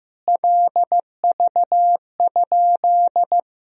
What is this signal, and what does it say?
Morse code, heard as a keyed tone, about 15 wpm, 700 Hz. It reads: LV?
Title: Sentences from Top 400 Words and Abbreviations 15wpm